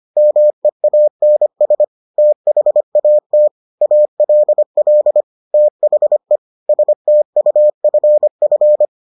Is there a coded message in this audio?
Yes: Morse code from 0.2 to 8.9 s, heard as a keyed tone, about 25 wpm, 600 Hz. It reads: MEANS THAT ALL THE STUFF